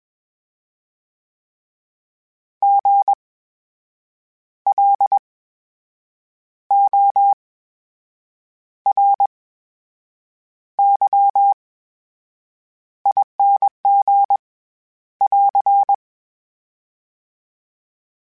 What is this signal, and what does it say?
Morse code, heard as a keyed tone, about 21 wpm, spaced as for 7 wpm, 800 Hz. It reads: GLORYING <AR>
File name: Special INGs-mp3s / Special INGs-mp3s / VerySlowToFaster Ing @20